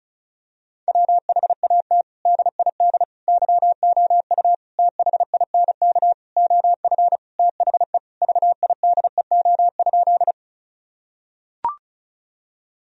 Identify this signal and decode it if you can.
Morse code, heard as a keyed tone, about 36 wpm, 700 Hz. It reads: WHAT DID YOU THINK OF THE VIDEO?
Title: Sentences from Top 1000 Words - Review Part 4 - 35wpm